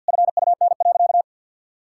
Morse code, heard as a keyed tone, about 50 wpm, 700 Hz. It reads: FUN.
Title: Sentences from Top 1600 Words 50wpm